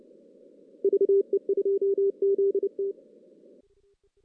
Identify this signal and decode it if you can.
Morse code, heard as a keyed tone, about 26 wpm, 410 Hz. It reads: VE2ZT
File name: 2016 JAN CW NAQP / 2016 Jan naqp k5ot - 0-6